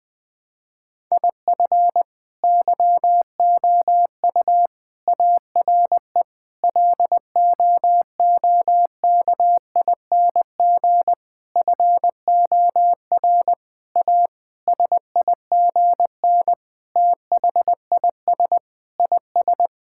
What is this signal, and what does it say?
Morse code, heard as a keyed tone, about 20 wpm, 700 Hz. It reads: IF YOU ARE LOOKING FOR A SIGN THIS IS
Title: Sentences with 6-Word Phrases - 20wpm